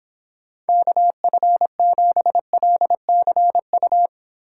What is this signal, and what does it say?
Morse code, heard as a keyed tone, about 26 wpm, 700 Hz. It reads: KF7LCU